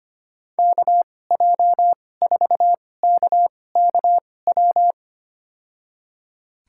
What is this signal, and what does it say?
Morse code, heard as a keyed tone, about 25 wpm, spaced as for 18 wpm, 700 Hz. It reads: KJ4KKW
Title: USA Call Signs - 2x Character Spacing - 25wpm